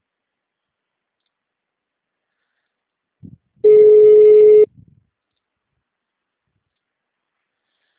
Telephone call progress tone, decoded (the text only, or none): ringback tone